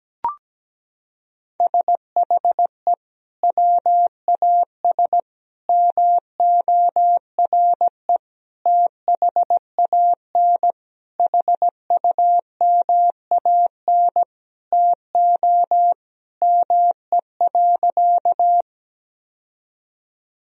Morse code, heard as a keyed tone, about 17 wpm, 700 Hz. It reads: SHE WAS MORE THAN HUMAN TO ME.
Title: Sentences from Top 1000 Words - Review Part 3 - 17wpm